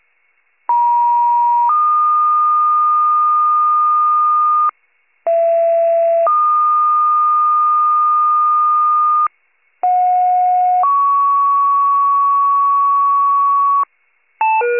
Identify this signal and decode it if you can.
Two-tone page, heard as a one-second tone, then a three-second tone, about 950 then 1220 Hz, about 670 then 1220 Hz, about 730 then 1120 Hz. It is heard from 0.7 to 13.8 s.
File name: pager / 21 RESCUE / wcalert